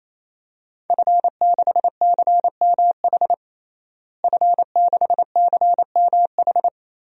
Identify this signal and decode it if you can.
Morse code, heard as a keyed tone, about 28 wpm, 700 Hz. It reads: F6CMH F6CMH
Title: DXCC-100 Call Signs - 5 Characters - Repeated 3x - 28wpm